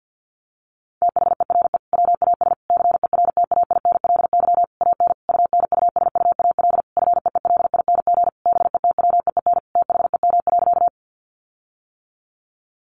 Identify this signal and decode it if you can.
Morse code, heard as a keyed tone, about 50 wpm, 700 Hz. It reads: THERE WAS CERTAINLY AN UNUSUAL FEELING BETWEEN THEM.
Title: Sentences from Top 1400 Words 50wpm